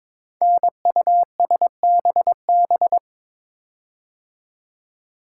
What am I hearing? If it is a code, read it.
Morse code, heard as a keyed tone, about 22 wpm, 700 Hz. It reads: NUSBB
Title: Code Groups - 5 Letters - 22wpm